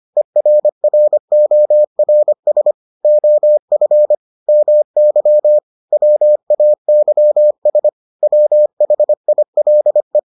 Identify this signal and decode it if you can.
Morse code, heard as a keyed tone, about 25 wpm, 600 Hz. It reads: ERRORS OF MY WAYS WHILE